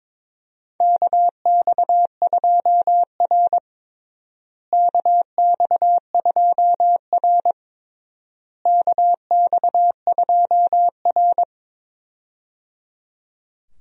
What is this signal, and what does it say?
Morse code, heard as a keyed tone, about 22 wpm, 700 Hz. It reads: KX2R KX2R KX2R
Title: USA Call Signs - 4 Characters - Repeated 3x - 22wpm